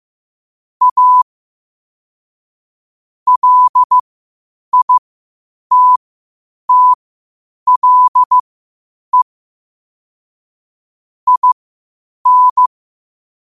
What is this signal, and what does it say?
Morse code, heard as a keyed tone, about 15 wpm, spaced as for 8 wpm, 1000 Hz. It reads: A LITTLE IN
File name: qotdfiles / 8wpm